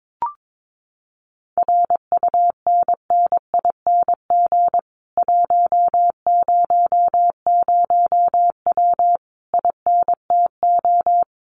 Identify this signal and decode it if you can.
Morse code, heard as a keyed tone, about 22 wpm, 700 Hz. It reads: RUNNING 100W INTO
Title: Common QSO Phrases Encore 22wpm